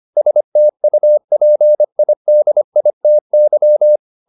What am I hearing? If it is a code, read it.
Morse code, heard as a keyed tone, about 25 wpm, 600 Hz. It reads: STUPIDITY